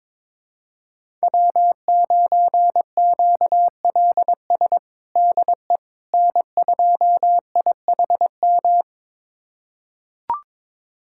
Morse code, heard as a keyed tone, about 22 wpm, 700 Hz. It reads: W9QLS DE N2IHM
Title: USA Callsign DE Callsign - 5 Characters - 22wpm